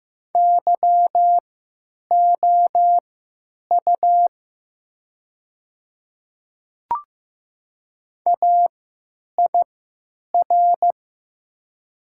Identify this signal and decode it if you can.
Morse code, heard as a keyed tone, about 15 wpm, spaced as for 9 wpm, 700 Hz. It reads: YOU AIR
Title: 3x Character Spacing - Top 200 Words - 15wpm